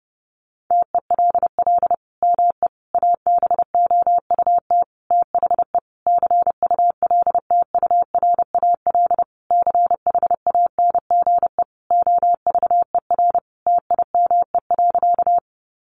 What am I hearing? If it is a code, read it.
Morse code, heard as a keyed tone, about 30 wpm, 700 Hz. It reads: TELL ME ABOUT THE CULTURAL CHANGE OVER TIME.
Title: Sentences from Top 1500 Words 30wpm